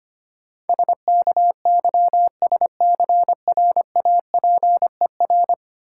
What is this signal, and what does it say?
Morse code, heard as a keyed tone, about 25 wpm, 700 Hz. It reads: SKYSCRAPER
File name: Easy Words - Speed-Racing with Hints - 25wpm